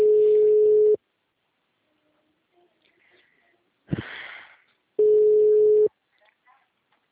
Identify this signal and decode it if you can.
Ringback tone (Europe, 425 Hz), with speech.
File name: out- -2013- - - .65524